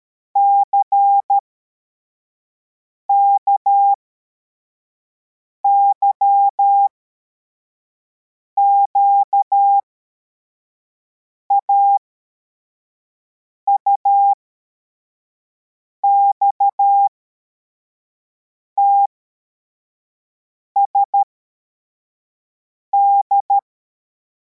Morse code, heard as a keyed tone, about 13 wpm, spaced as for 4 wpm, 800 Hz. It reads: CKYQA UXTSD
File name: formation / cw / Text84